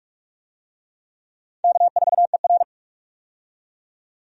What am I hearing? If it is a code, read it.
Morse code, heard as a keyed tone, about 42 wpm, 700 Hz. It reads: K4ER
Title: USA Call Signs - 4 Characters - 45wpm